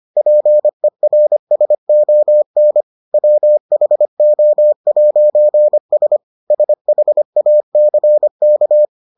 Morse code, heard as a keyed tone, about 25 wpm, 600 Hz. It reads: PERSON WHO'S SHACK